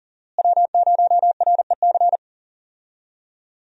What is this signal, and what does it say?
Morse code, heard as a keyed tone, about 38 wpm, 700 Hz. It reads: W0REC
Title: USA Call Signs - 5 Characters - Repeated 3x - 40wpm